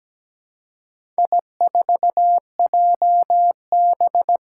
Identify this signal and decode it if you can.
Morse code, heard as a keyed tone, about 17 wpm, 700 Hz. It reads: I4JB
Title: Four Challenging Letter-Number Combinations 17wpm